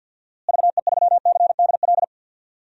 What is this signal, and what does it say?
Morse code, heard as a keyed tone, about 53 wpm, 700 Hz. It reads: VE3CDL
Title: DXCC-100 Call Signs - 6 Characters - Repeated 2x - 50wpm